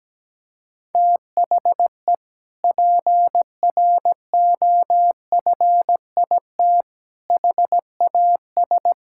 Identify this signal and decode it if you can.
Morse code, heard as a keyed tone, about 17 wpm, 700 Hz. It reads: THE PROFIT HAS
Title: Sentences from Top 1000 Words - Review Part 5 - 17wpm